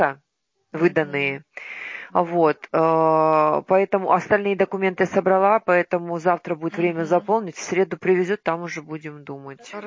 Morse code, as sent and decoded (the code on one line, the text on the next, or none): none